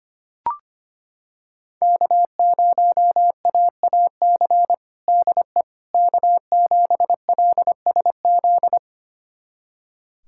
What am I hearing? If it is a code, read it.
Morse code, heard as a keyed tone, about 25 wpm, 700 Hz. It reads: K0AAC DE K7LSZ
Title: USA Callsign DE Callsign - 5 Characters - 25wpm